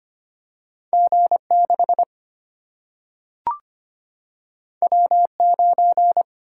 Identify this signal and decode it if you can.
Morse code, heard as a keyed tone, about 25 wpm, 700 Hz. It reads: G 6 W 9
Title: Single-Letter Single-Number 25wpm